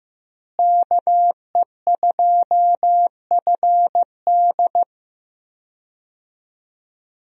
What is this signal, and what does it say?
Morse code, heard as a keyed tone, about 15 wpm, 700 Hz. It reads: KE2FD
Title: USA Call Signs - 5 Characters - Repeated 2x - 15wpm